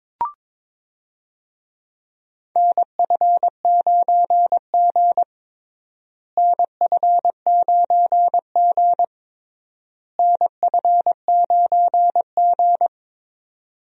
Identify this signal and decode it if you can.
Morse code, heard as a keyed tone, about 21 wpm, 700 Hz. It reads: NF9G NF9G NF9G
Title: USA Call Signs - 4 Characters - Repeated 3x - 22wpm